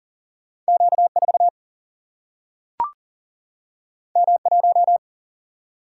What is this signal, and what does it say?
Morse code, heard as a keyed tone, about 40 wpm, 700 Hz. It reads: Q 4 M 1
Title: Single-Letter Single-Number 40wpm